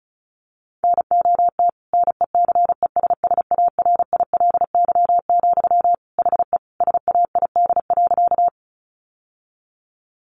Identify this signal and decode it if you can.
Morse code, heard as a keyed tone, about 34 wpm, 700 Hz. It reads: NOT NECESSARILY, HE SAID.
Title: Sentences from Top 1400 Words 35wpm